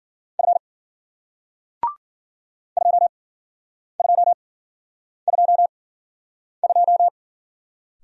Morse code, heard as a keyed tone, about 53 wpm, spaced as for 9 wpm, 700 Hz. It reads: F 2222